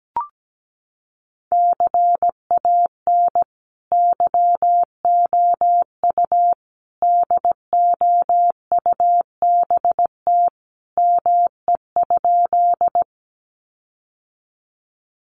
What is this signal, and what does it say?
Morse code, heard as a keyed tone, about 17 wpm, 700 Hz. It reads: CAN YOU DOUBT ME?
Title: Sentences from Top 1500 Words 17wpm